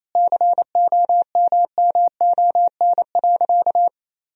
Morse code, heard as a keyed tone, about 28 wpm, 700 Hz. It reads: COMMON.
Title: Sentences from Top 1000 Words - Review Part 3 - 28wpm